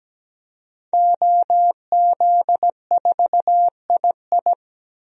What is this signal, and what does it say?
Morse code, heard as a keyed tone, about 17 wpm, 700 Hz. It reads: OZ4II
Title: DXCC-100 Call Signs - 5 Characters - 17wpm